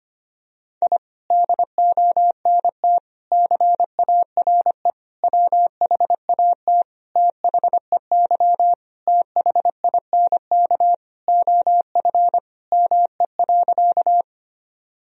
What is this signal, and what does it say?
Morse code, heard as a keyed tone, about 25 wpm, 700 Hz. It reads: I DONT CARE WHAT THEY THINK OF ME.
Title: Sentences from Top 1000 Words - Review Part 1 - 25wpm